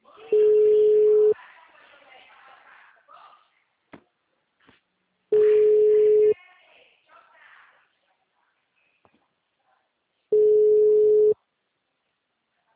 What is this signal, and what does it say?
Ringback tone (Europe, 425 Hz), with speech.